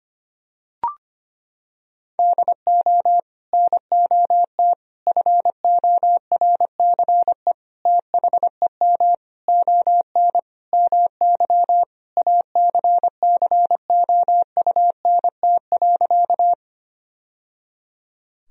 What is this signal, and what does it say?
Morse code, heard as a keyed tone, about 25 wpm, 700 Hz. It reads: DO NOT FORCE THEM ON MY ACCOUNT.
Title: Sentences from Top 1000 Words - Review Part 3 - 25wpm